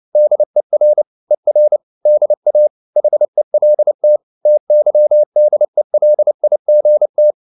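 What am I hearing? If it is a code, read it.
Morse code, heard as a keyed tone, about 29 wpm, 600 Hz. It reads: DER ER DA HELT TYDELIGT
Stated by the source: Fast (QRQ) Morse at 29 wpm, from CW med Gnister.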